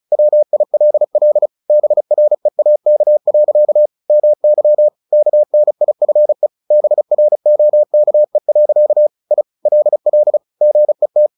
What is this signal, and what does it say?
Morse code, heard as a keyed tone, about 34 wpm, 600 Hz. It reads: WILL BREAK. MY KNIFE BROKE. I LL GET
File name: the old man and the sea 2 35wpm